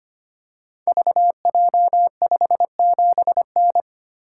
Morse code, heard as a keyed tone, about 25 wpm, 690 Hz. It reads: VJ57N
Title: Code Groups - 5 Letter:Numbers - 25wpm